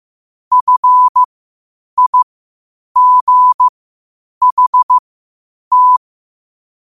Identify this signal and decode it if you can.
Morse code, heard as a keyed tone, about 15 wpm, spaced as for 8 wpm, 1000 Hz. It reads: FIGHT
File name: qotdfiles / 8wpm